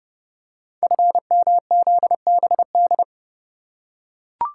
Morse code, heard as a keyed tone, about 30 wpm, 700 Hz. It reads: FMZBD
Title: Code Groups - 5 Letters - 30wpm